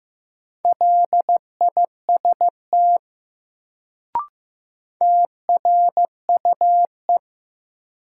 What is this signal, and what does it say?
Morse code, heard as a keyed tone, about 15 wpm, 700 Hz. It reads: LIST TRUE
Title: Sentences from Top 400 Words - New Word Drill - 15wpm